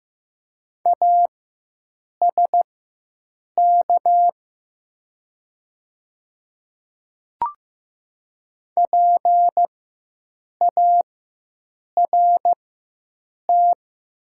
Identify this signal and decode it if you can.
Morse code, heard as a keyed tone, about 15 wpm, spaced as for 7 wpm, 690 Hz. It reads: ASK PART
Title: 4x Character Spacing - Top 200 Words - 15wpm